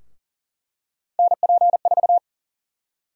Morse code, heard as a keyed tone, about 40 wpm, 700 Hz. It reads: NP4
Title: USA Call Sign Prefixes - Repeated 2x - 40wpm